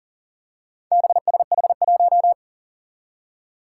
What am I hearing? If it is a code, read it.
Morse code, heard as a keyed tone, about 40 wpm, 700 Hz. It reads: BSH1
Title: Four Challenging Letter-Number Combinations 40wpm